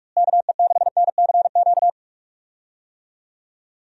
Morse code, heard as a keyed tone, about 43 wpm, 700 Hz. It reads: KE6NCQ
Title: USA Call Signs - 6 Characters - Repeated 2x - 45wpm